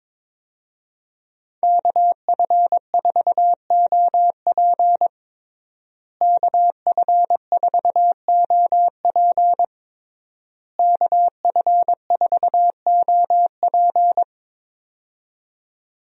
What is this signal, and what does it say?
Morse code, heard as a keyed tone, about 22 wpm, 700 Hz. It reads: KF4OP KF4OP KF4OP
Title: USA Call Signs - 5 Characters - Repeated 3x - 22wpm